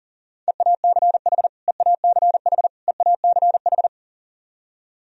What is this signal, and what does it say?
Morse code, heard as a keyed tone, about 40 wpm, 700 Hz. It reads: EACH EACH EACH
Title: Top 100 Words Repeated 40wpm